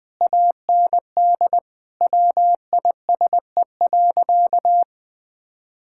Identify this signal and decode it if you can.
Morse code, heard as a keyed tone, about 21 wpm, 700 Hz. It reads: AND WISE.
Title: Sentences from Top 1600 Words 20wpm